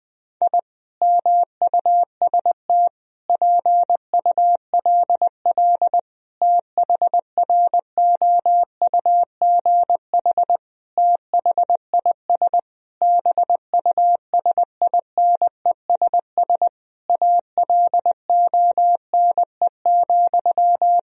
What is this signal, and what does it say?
Morse code, heard as a keyed tone, about 21 wpm, 700 Hz. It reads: I MUST PULL THROUGH THIS BUSINESS ALONE,
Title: Sentences from Top 1000 Words - Review Part 2 - 20wpm